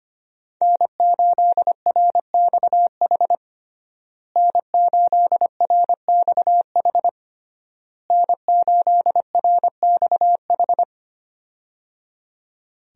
Morse code, heard as a keyed tone, about 25 wpm, 700 Hz. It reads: N8RXH N8RXH N8RXH
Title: USA Call Signs - 5 Characters - Repeated 3x - 25wpm